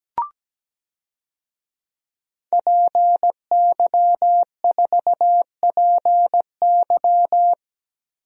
Morse code, heard as a keyed tone, about 17 wpm, 700 Hz. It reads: PY4PY
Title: DXCC-100 Call Signs - 5 Characters - 17wpm